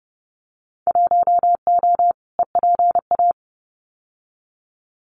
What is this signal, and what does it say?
Morse code, heard as a keyed tone, about 30 wpm, 700 Hz. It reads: 1O EPA
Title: Winter Field Day Exchanges 30wpm